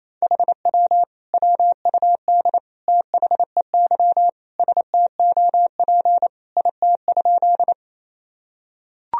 Morse code, heard as a keyed tone, about 28 wpm, 700 Hz. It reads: HW WUD THEY STOP IT?
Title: Sentences from Top 300 Words and Abbreviations 28wpm